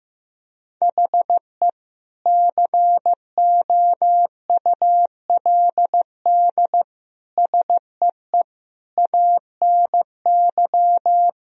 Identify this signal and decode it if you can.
Morse code, heard as a keyed tone, about 16 wpm, 700 Hz. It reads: HE COULD SEE ANY
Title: Sentences from Top 1000 Words - Review Part 3 - 15wpm